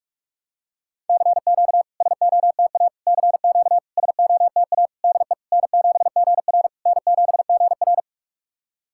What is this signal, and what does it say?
Morse code, heard as a keyed tone, about 44 wpm, 700 Hz. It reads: CQ SOTA CQ SOTA DE N7GR N7GR
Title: SOTA QSO Practice 45wpm